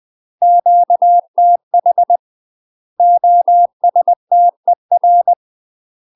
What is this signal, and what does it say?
Morse code, heard as a keyed tone, about 20 wpm, 700 Hz. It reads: QTH OSTER
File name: UK MASTER QSOs TRAINING / 20WPM / UK MASTER QSOs 0063